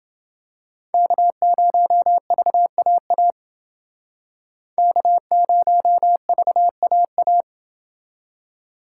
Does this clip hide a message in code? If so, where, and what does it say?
Yes: Morse code from 0.9 to 7.4 s, heard as a keyed tone, about 29 wpm, 700 Hz. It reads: K0VAA K0VAA